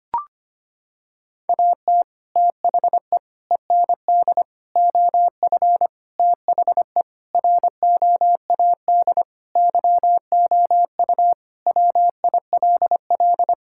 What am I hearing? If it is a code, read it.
Morse code, heard as a keyed tone, about 25 wpm, 700 Hz. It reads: AT THE END OF THE ROAD YOU WILL
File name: practice / 25wpm / Sentences with 3-Word Phrases - 25wpm